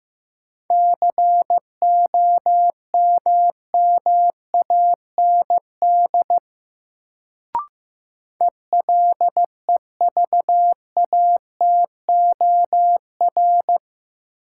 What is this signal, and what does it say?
Morse code, heard as a keyed tone, about 15 wpm, 700 Hz. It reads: COMMAND ELEVATOR
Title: Sentences from Top 1600 Words - New Word Drill Alt - 15wpm